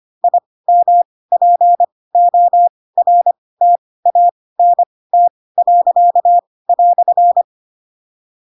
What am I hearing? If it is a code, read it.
Morse code, heard as a keyed tone, about 25 wpm, spaced as for 18 wpm, 700 Hz. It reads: IMPORTANT."